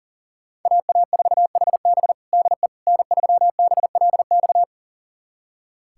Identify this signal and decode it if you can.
Morse code, heard as a keyed tone, about 39 wpm, 700 Hz. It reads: AA4HB DE N3BLX
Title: USA Callsign DE Callsign - 5 Characters - 40wpm